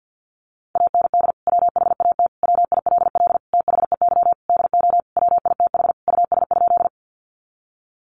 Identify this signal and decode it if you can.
Morse code, heard as a keyed tone, about 50 wpm, 700 Hz. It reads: AND WHAT WILL THEY DO WITH US?
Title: Sentences from Top 200 Words 50wpm